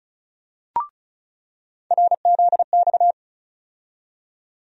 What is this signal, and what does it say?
Morse code, heard as a keyed tone, about 36 wpm, 700 Hz. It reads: RZX
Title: USA Call Sign Postfixes - 3 Letters 35wpm